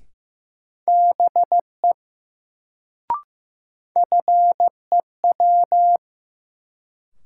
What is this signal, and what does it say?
Morse code, heard as a keyed tone, about 15 wpm, 700 Hz. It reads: BE FEW
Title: Sets of 2 Words - New Word Drill - 15wpm